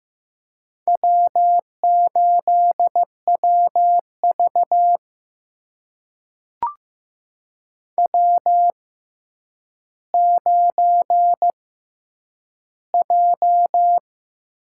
Morse code, heard as a keyed tone, about 15 wpm, spaced as for 9 wpm, 700 Hz. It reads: W8WV W9J
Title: USA Call Signs - 4 Characters - 6x Character Spacing - 15wpm